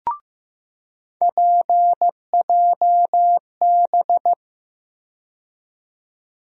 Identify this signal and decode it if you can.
Morse code, heard as a keyed tone, about 15 wpm, 700 Hz. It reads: PJB
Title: USA Call Sign Postfixes - 3 Letters 15wpm